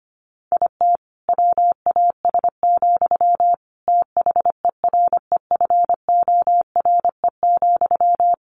Morse code, heard as a keyed tone, about 25 wpm, 700 Hz. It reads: IT WAS, THEREFORE,
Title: Sentences from Top 1400 Words 25wpm